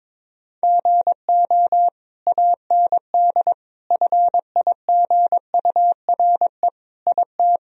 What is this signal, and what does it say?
Morse code, heard as a keyed tone, about 21 wpm, 700 Hz. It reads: GO AND FIGURE IT
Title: Sentences from Top 1000 Words - Review Part 2 - 22wpm